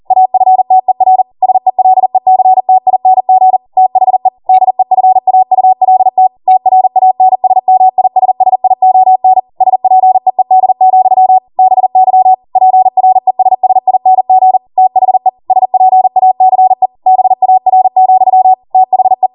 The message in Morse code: AFTER SELECTING THE DEFAULT TRANSMISSION SPEED, BY PRESSING THE SPACE BAR, THE